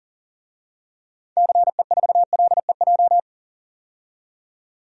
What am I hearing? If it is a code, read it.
Morse code, heard as a keyed tone, about 38 wpm, 700 Hz. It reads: CE4LEJ
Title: DXCC-100 Call Signs - 6 Characters - 40wpm